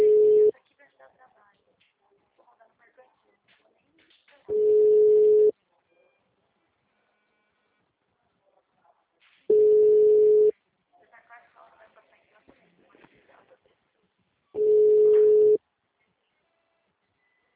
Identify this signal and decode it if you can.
Ringback tone (Europe, 425 Hz), with speech.